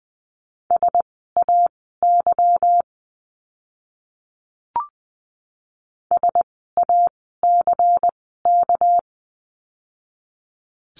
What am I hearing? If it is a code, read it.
Morse code, heard as a keyed tone, about 20 wpm, spaced as for 14 wpm, 700 Hz. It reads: SAY SACK
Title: 199 Lesson 32f - Letter k - Words 20wpm